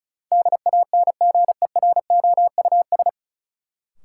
Morse code, heard as a keyed tone, about 34 wpm, 700 Hz. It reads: DANGEROUS
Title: Sentences from Top 1200 Words - New Word Drill - 35wpm